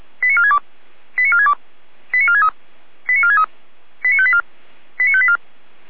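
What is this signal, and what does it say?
Five-tone selective call (ZVEI), heard as quick runs of five tones, tones of about 70 ms. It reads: 89462 89462 89463 89463 89584 89584
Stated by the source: ZVEI five-tone selective call 89436, received on an SDR.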